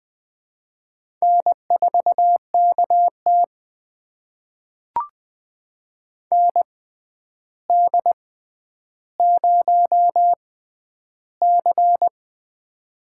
Morse code, heard as a keyed tone, about 20 wpm, spaced as for 10 wpm, 700 Hz. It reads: N4KT ND0C
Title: USA Call Signs - 4 Characters - 6x Character Spacing - 20wpm